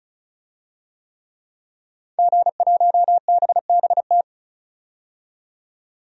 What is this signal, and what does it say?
Morse code, heard as a keyed tone, about 34 wpm, 700 Hz. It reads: G1BBT
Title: DXCC-100 Call Signs - 5 Characters - 35wpm